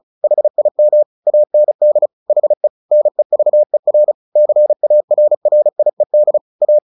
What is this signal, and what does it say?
Morse code, heard as a keyed tone, about 35 wpm, 600 Hz. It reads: HIM AND HE NEVER CARRIED A
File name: the old man and the sea / the old man and the sea 1 35wpm